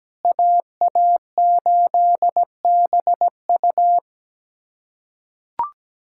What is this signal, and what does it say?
Morse code, heard as a keyed tone, about 17 wpm, 700 Hz. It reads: AA8BU
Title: USA Callsign DE Callsign - 5 Characters - 17wpm